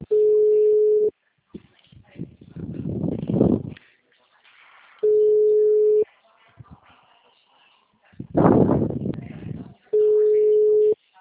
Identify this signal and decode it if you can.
Ringback tone (Europe, 425 Hz), with speech.